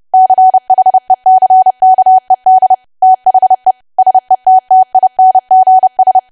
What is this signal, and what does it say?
Morse code, heard as a keyed tone, about 30 wpm, 750 Hz. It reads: CHECKED THE SETTINGS
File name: cwa / W1AW-Level3 / 30WPM